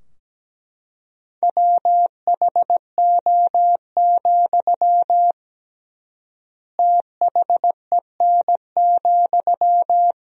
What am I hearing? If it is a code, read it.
Morse code, heard as a keyed tone, about 17 wpm, 700 Hz. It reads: WHO, THEN,
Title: 3x Word-Spacing - Sentences from Top 100 Words - 17wpm